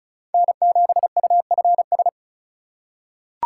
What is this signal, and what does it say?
Morse code, heard as a keyed tone, about 36 wpm, 700 Hz. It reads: N7UFS
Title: USA Call Signs - 5 Characters - 3x Character Spacing - 35wpm